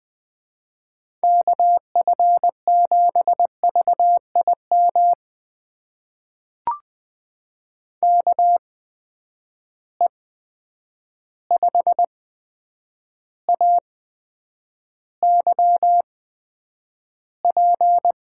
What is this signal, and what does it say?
Morse code, heard as a keyed tone, about 20 wpm, spaced as for 9 wpm, 700 Hz. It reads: KF7VIM KE5AYP
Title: USA Call Signs - 6 Characters - 8x Character Spacing - 20wpm